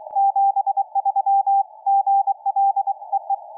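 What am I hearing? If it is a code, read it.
Morse code, heard as a keyed tone, about 24 wpm, 770 Hz, with radio noise. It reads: 73 GL EE
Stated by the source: QRP CW contact between UR4MCK/P and RD2A during the WAKE Up! QRP Sprint, recorded off air by an RU-QRP Club member.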